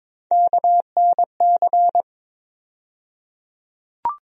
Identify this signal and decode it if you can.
Morse code, heard as a keyed tone, about 22 wpm, 700 Hz. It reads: KNC
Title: USA Call Sign Postfixes - 3 Letters 22wpm